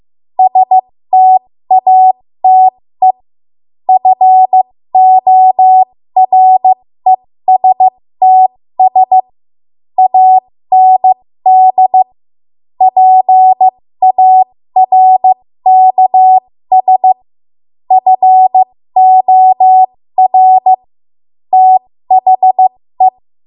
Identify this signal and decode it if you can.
Morse code, heard as a keyed tone, about 15 wpm, spaced as for 13 wpm, 750 Hz. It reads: STATE FORESTS AND PARKS FOR THE
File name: Morse / 13WPM